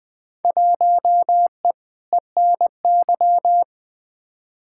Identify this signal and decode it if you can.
Morse code, heard as a keyed tone, about 20 wpm, 700 Hz. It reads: 1E ENY
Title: Field Day Exchanges - 20wpm